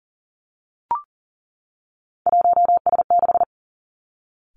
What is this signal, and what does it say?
Morse code, heard as a keyed tone, about 42 wpm, 700 Hz. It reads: 1S6
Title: Three Challenging Letter-Number Combinations 40wpm